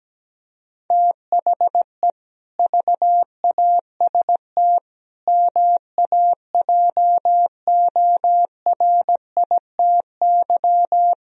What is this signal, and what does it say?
Morse code, heard as a keyed tone, about 17 wpm, 700 Hz. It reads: THE VAST MAJORITY
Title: Speed-Racing - Sets of 4 Words 17wpm